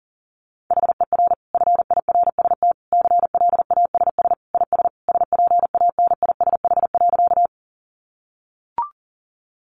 Morse code, heard as a keyed tone, about 40 wpm, 700 Hz. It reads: HER FIRST CLASS IS SPANISH.
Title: Additional Easy Phrases 40wpm